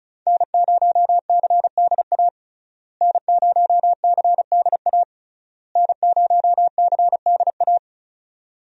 Morse code, heard as a keyed tone, about 36 wpm, 700 Hz. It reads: N0CDA N0CDA N0CDA
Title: USA Call Signs - 5 Characters - Repeated 3x - 35wpm